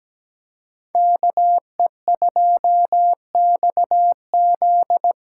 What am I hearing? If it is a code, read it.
Morse code, heard as a keyed tone, about 17 wpm, 700 Hz. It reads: KE2XZ
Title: USA Call Signs - 5 Characters - 3x Character Spacing - 17wpm